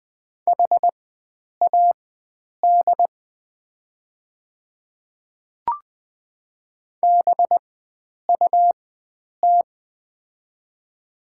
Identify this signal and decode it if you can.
Morse code, heard as a keyed tone, about 20 wpm, spaced as for 9 wpm, 690 Hz. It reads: HAD BUT
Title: 4x Character Spacing - Top 100 Words - 20wpm